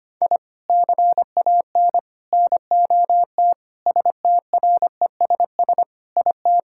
Morse code, heard as a keyed tone, about 25 wpm, 700 Hz. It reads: I CAN NOT STRESS IT
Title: Sentences from Top 1000 Words - Review Part 6 - 25wpm